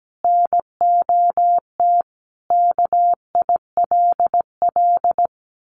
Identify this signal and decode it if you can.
Morse code, heard as a keyed tone, about 17 wpm, 700 Hz. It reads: NOT KILL
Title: Sentences from Top 1400 Words 17wpm